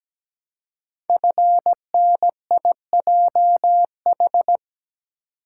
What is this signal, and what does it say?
Morse code, heard as a keyed tone, about 17 wpm, 700 Hz. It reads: FNIJH
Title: Code Groups - 5 Letters - 17wpm